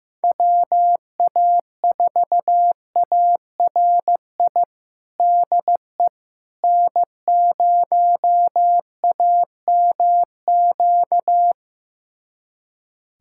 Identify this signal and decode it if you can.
Morse code, heard as a keyed tone, about 15 wpm, 700 Hz. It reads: WA4ARI DE N0AMQ
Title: USA Callsign DE Callsign 15wpm